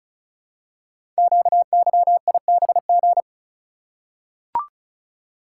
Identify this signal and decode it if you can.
Morse code, heard as a keyed tone, about 36 wpm, 690 Hz. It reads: QYIBG
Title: Code Groups - 5 Letters - 35wpm